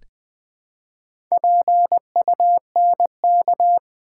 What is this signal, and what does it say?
Morse code, heard as a keyed tone, about 20 wpm, 700 Hz. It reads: PUNK
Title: Build A Word - 1 Letter At A Time - 3 to 5-Letter Words - 20wpm